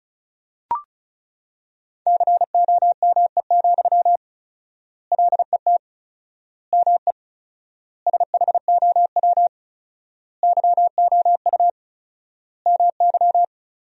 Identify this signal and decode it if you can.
Morse code, heard as a keyed tone, about 36 wpm, 700 Hz. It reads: COME, LET ME SHOW YOU MY
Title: 4x Word-Spacing - Sentences from Top 500 Words - 35wpm